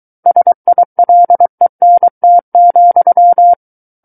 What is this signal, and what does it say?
Morse code, heard as a keyed tone, about 23 wpm, 700 Hz. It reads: SILENT,
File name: Book One / 14-Chapter 14 - 23 WPM